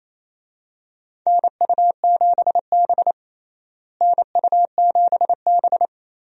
Morse code, heard as a keyed tone, about 28 wpm, 700 Hz. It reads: NU7B NU7B
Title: USA Call Signs - 4 Characters - Repeated 3x - 28wpm